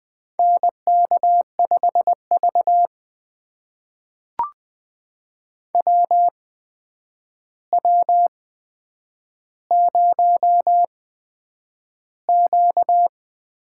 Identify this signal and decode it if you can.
Morse code, heard as a keyed tone, about 20 wpm, spaced as for 9 wpm, 700 Hz. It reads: NK5V WW0Q
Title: USA Call Signs - 8x Character Spacing - 20wpm